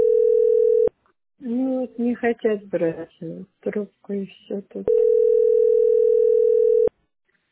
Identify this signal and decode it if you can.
Ringback tone (North America), 440 + 480 Hz, with speech.
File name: 07d122f6-78cb-46cf-b133